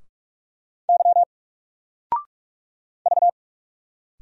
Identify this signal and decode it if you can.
Morse code, heard as a keyed tone, about 46 wpm, 700 Hz. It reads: Y V
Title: Single Letter-Number 45wpm